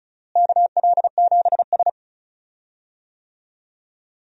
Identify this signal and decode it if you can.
Morse code, heard as a keyed tone, about 34 wpm, 700 Hz. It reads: KL7S
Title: DXCC-100 Call Signs - 4 Characters - Repeated 3x - 35wpm